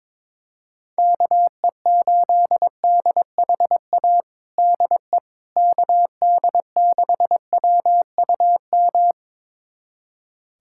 Morse code, heard as a keyed tone, about 22 wpm, 700 Hz. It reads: KE8DHA DE KD6WUM
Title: USA Callsign DE Callsign - 6 Characters - 22wpm